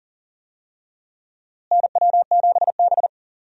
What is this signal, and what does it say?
Morse code, heard as a keyed tone, about 40 wpm, 700 Hz. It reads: NW7B
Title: USA Call Signs - 4 Characters - Repeated 2x - 40wpm